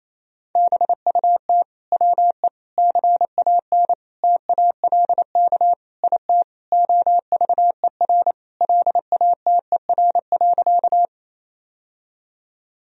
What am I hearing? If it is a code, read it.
Morse code, heard as a keyed tone, about 28 wpm, 700 Hz. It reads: BUT WE CAN TALK IT OVER LATER.
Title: Sentences from Top 1000 Words - Review Part 4 - 28wpm